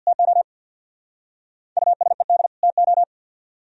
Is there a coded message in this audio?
Yes: Morse code from 0.1 to 3.0 s, heard as a keyed tone, about 65 wpm, 700 Hz. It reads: TO USED TO